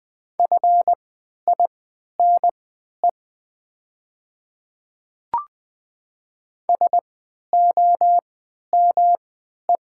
Morse code, heard as a keyed tone, about 20 wpm, spaced as for 11 wpm, 700 Hz. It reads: FINE SOME
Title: 3x Character Spacing - Top 500 Words - 20wpm